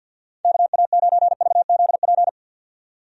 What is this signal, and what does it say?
Morse code, heard as a keyed tone, about 51 wpm, 700 Hz. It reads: KA9VZP
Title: USA Call Signs - 6 Characters - 50wpm